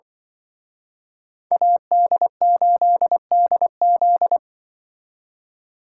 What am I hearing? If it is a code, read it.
Morse code, heard as a keyed tone, about 24 wpm, 700 Hz. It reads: AD8DZ
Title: Speed Racing - USA 2x2 Call Signs 20wpm